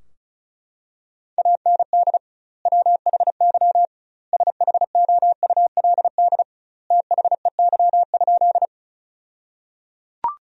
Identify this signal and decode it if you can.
Morse code, heard as a keyed tone, about 34 wpm, 700 Hz. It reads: AND WHY SHOULD THEY?
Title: 2x Word-Spacing - Sentences from Top 300 Words - 35wpm